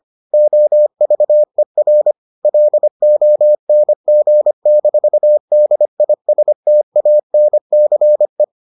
Morse code, heard as a keyed tone, about 25 wpm, 600 Hz. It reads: OVER LONG-DISTANCE